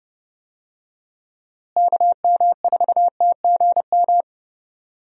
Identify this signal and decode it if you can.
Morse code, heard as a keyed tone, about 29 wpm, 700 Hz. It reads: KM4TGM
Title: USA Call Signs - 6 Characters - Repeated 2x - 30wpm